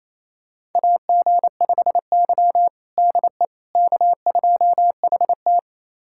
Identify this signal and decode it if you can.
Morse code, heard as a keyed tone, about 28 wpm, 700 Hz. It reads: AG5Y DE K2HT
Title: USA Callsign DE Callsign - 4 Characters - 28wpm